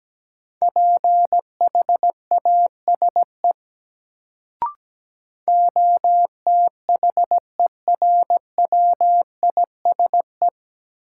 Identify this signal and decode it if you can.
Morse code, heard as a keyed tone, about 17 wpm, 700 Hz. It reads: PHASE OTHERWISE
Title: Sentences from Top 1200 Words - New Word Drill - 17wpm